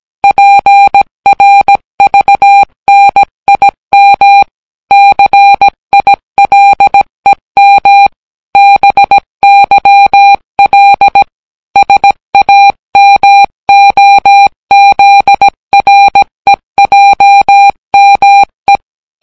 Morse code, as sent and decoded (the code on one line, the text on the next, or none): .--. .-. ...- -. .. -- / -.-. .. .-.. -- / -... -.-- .-.. / ... .- -- --- --.. .-. .--- -- .
PRVNIM CILEM BYL SAMOZREJME